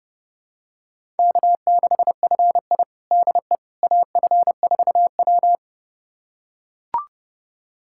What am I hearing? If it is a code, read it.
Morse code, heard as a keyed tone, about 31 wpm, 700 Hz. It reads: K6FI DE AF4W
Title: USA Callsign DE Callsign - 4 Characters - 30wpm